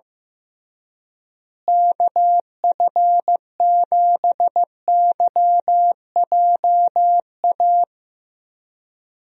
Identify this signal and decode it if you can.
Morse code, heard as a keyed tone, about 15 wpm, 700 Hz. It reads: KF7YJA
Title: USA Call Signs - 6 Characters - Repeated 3x - 15wpm